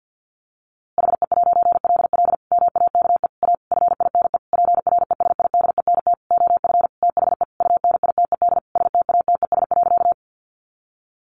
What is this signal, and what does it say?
Morse code, heard as a keyed tone, about 50 wpm, 700 Hz. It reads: HE'LL MAKE A FINE PRESIDENT OF THE UNITED STATES.
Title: A Sound Of Thunder 50wpm